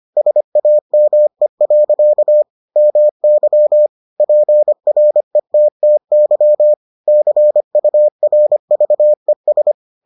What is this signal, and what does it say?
Morse code, heard as a keyed tone, about 25 wpm, 600 Hz. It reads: SAME. MY PRETTY CURVES